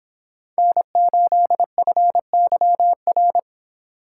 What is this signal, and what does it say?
Morse code, heard as a keyed tone, about 31 wpm, 700 Hz. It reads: N8FYR